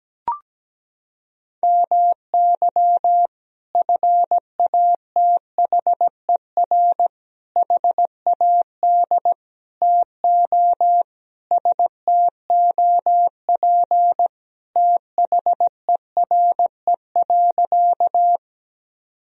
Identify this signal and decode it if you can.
Morse code, heard as a keyed tone, about 17 wpm, 700 Hz. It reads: MY FATHER HAD TO STOP THERE.
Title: Sentences from Top 1000 Words - Review Part 1 - 17wpm